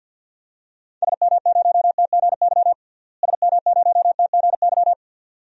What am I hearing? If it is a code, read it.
Morse code, heard as a keyed tone, about 47 wpm, 700 Hz. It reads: SM0TGY SM0TGY
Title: DXCC-100 Call Signs - 6 Characters - Repeated 2x - 50wpm